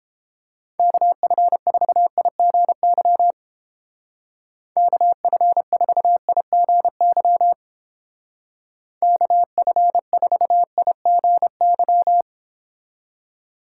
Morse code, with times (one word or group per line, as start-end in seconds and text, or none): 0.8-3.3 KF4IGY
4.8-7.5 KF4IGY
9.0-12.2 KF4IGY